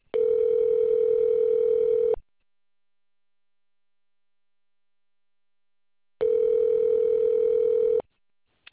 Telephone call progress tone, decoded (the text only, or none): ringback tone